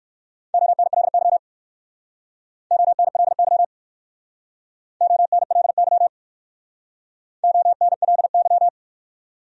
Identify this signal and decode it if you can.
Morse code, heard as a keyed tone, about 62 wpm, 700 Hz. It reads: ONLY ONLY ONLY ONLY